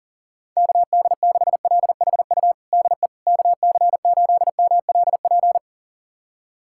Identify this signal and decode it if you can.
Morse code, heard as a keyed tone, about 40 wpm, 700 Hz. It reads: KD6LHU DE KC8MLP
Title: USA Callsign DE Callsign 40wpm